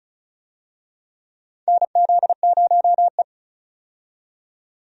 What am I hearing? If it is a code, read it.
Morse code, heard as a keyed tone, about 34 wpm, 700 Hz. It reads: NZ0E
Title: USA Call Signs - 4 Characters - 35wpm